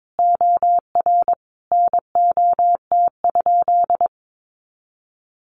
Morse code, heard as a keyed tone, about 22 wpm, 700 Hz. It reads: OR NOT?
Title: Sentences from Top 1300 Words 22wpm